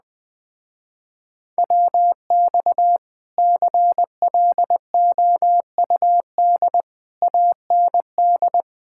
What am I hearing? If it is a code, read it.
Morse code, heard as a keyed tone, about 21 wpm, 700 Hz. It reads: WX CLOUD AND